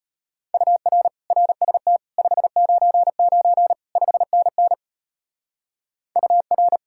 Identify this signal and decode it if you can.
Morse code, heard as a keyed tone, about 38 wpm, 700 Hz. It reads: UR RST 599 5NN UR